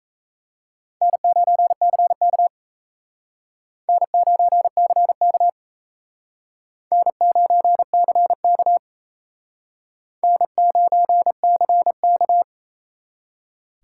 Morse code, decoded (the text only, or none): N9CK N9CK N9CK N9CK